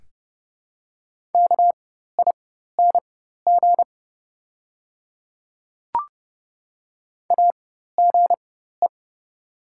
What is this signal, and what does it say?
Morse code, heard as a keyed tone, about 30 wpm, spaced as for 14 wpm, 700 Hz. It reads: KING AGE
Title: 4x Character Spacing - Top 500 Words - 30wpm